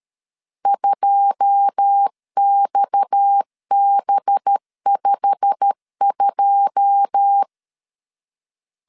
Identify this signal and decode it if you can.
Morse code, heard as a keyed tone, about 13 wpm, 800 Hz. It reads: 2XB52
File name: LEZIONI / 9 2 12wpm 2301